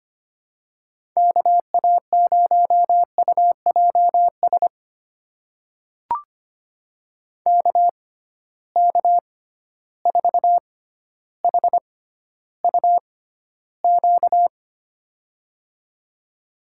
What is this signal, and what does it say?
Morse code, heard as a keyed tone, about 25 wpm, spaced as for 13 wpm, 700 Hz. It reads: KA0UJS KK4HUQ
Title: USA Call Signs - 6 Characters - 6x Character Spacing - 25wpm